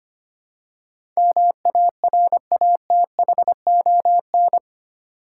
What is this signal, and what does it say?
Morse code, heard as a keyed tone, about 25 wpm, 700 Hz. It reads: MARATHON
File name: Eight Letter Words - 25wpm